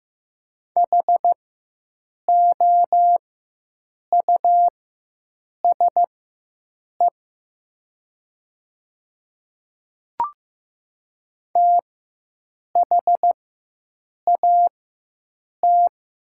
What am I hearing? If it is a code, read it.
Morse code, heard as a keyed tone, about 15 wpm, spaced as for 7 wpm, 700 Hz. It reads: HOUSE THAT
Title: 4x Character Spacing - Top 500 Words - 15wpm